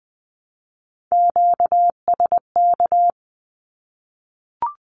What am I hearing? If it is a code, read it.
Morse code, heard as a keyed tone, about 20 wpm, 700 Hz. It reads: QSK
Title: Q-Codes Repeated 3x 20wpm